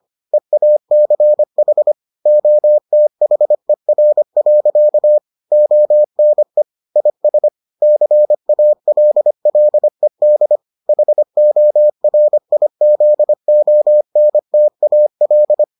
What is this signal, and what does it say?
Morse code, heard as a keyed tone, about 25 wpm, 600 Hz. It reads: EACH OTHER. ONE IS CALLED HORIZONTAL